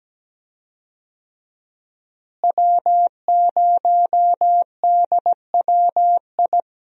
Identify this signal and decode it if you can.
Morse code, heard as a keyed tone, about 17 wpm, 700 Hz. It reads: W0DWI
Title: USA Call Signs - 5 Characters - Repeated 2x - 17wpm